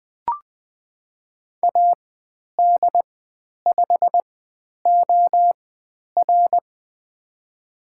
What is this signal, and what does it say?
Morse code, heard as a keyed tone, about 20 wpm, spaced as for 10 wpm, 700 Hz. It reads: AD5OR
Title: 095 Lesson 16g - Letter c - Call Signs 20wpm